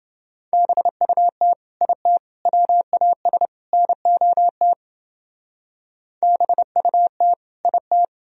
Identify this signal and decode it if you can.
Morse code, heard as a keyed tone, about 29 wpm, 700 Hz. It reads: BUT IT WAS NOT BUT IT